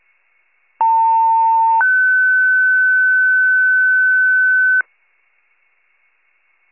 Two-tone page, heard as a one-second tone, then a three-second tone, about 900 then 1500 Hz. It is heard from 0.8 to 4.8 s.
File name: wcalert